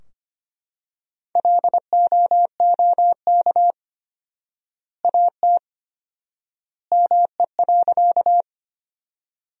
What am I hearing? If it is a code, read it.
Morse code, heard as a keyed tone, about 25 wpm, 700 Hz. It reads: LOOK AT ME.
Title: 4x Word-Spacing - Sentences from Top 200 Words - 25wpm